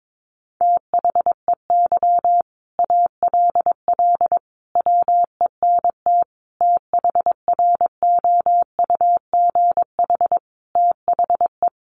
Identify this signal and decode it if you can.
Morse code, heard as a keyed tone, about 22 wpm, 700 Hz. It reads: THEY ALL WENT THROUGH THE
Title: Sentences from Top 1300 Words 22wpm